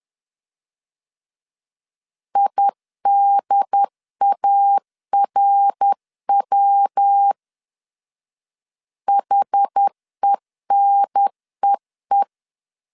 Morse code, heard as a keyed tone, about 10 wpm, 800 Hz. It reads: IDARW HENEE
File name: CW / DET / 1 3 1500C S A 10wpm 3447